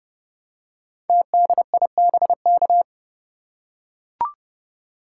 Morse code, heard as a keyed tone, about 30 wpm, 700 Hz. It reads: TDIBK
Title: Code Groups - 5 Letters - 30wpm